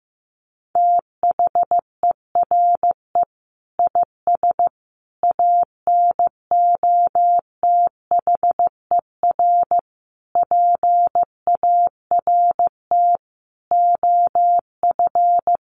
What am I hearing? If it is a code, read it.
Morse code, heard as a keyed tone, about 15 wpm, 700 Hz. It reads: THERE IS ANOTHER PART OF
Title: Sentences from Top 1300 Words 15wpm